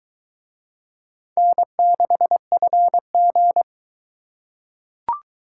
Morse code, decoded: N6FG

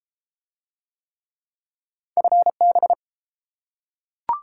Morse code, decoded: FB

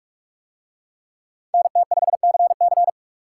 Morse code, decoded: NT5CC

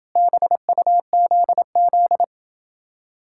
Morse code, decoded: BUZZ

